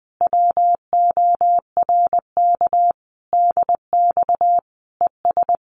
Morse code, heard as a keyed tone, about 20 wpm, 700 Hz. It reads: WORK DX ES
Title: Common QSO Phrases Encore 20wpm